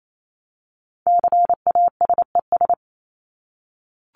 Morse code, heard as a keyed tone, about 28 wpm, 700 Hz. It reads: CASES
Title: ES:S Suffix - Top 100 Words with Base Words 28wpm